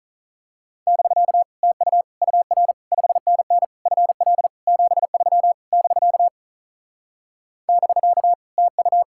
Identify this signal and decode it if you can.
Morse code, heard as a keyed tone, about 38 wpm, 700 Hz. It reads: <BK> TU UR 5NN FL 73 <BK> <BK> TU
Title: Speed-Racing Common QSO Phrases Encore - 35wpm